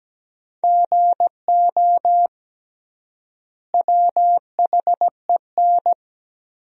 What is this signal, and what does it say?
Morse code, heard as a keyed tone, about 17 wpm, 700 Hz. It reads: GO WHEN
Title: 3x Word-Spacing - Sentences from Top 500 Words - 17wpm